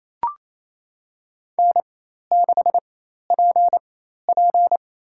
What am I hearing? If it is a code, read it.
Morse code, heard as a keyed tone, about 28 wpm, spaced as for 13 wpm, 700 Hz. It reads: N6PP
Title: USA Call Signs - 4x Character Spacing - 28wpm